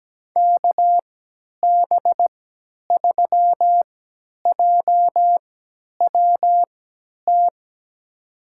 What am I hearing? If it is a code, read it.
Morse code, heard as a keyed tone, about 17 wpm, spaced as for 10 wpm, 700 Hz. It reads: KB3JWT
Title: USA Call Signs - 6 Characters - 3x Character Spacing - 17wpm